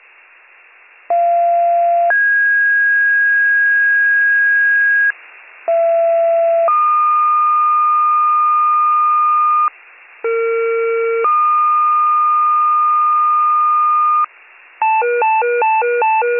Two-tone page, heard as a one-second tone, then a three-second tone, about 690 then 1650 Hz, about 670 then 1220 Hz, about 460 then 1220 Hz, with noise. It is heard from 1.1 to 14.2 s.